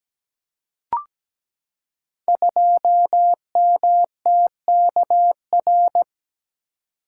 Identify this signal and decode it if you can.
Morse code, heard as a keyed tone, about 17 wpm, 700 Hz. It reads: 2MTKR
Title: Code Groups - 5 Letter:Numbers - 17wpm